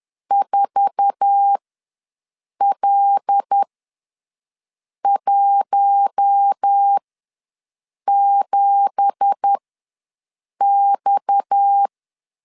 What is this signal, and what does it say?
Morse code, heard as a keyed tone, about 10 wpm, spaced as for 6 wpm, 800 Hz. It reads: 4L17X
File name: LEZIONI / 9 2867 ws 10wpm 4215